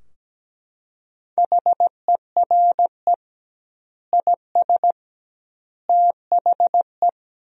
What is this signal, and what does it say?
Morse code, heard as a keyed tone, about 17 wpm, 700 Hz. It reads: HERE IS THE
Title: 2x Word-Spacing - Sentences from Top 300 Words - 17wpm